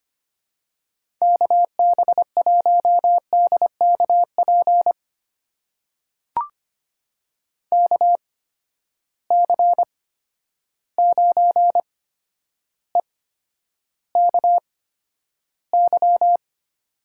Morse code, heard as a keyed tone, about 25 wpm, spaced as for 11 wpm, 700 Hz. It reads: KB1DKP KC9EKY
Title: USA Call Signs - 6 Characters - 8x Character Spacing - 25wpm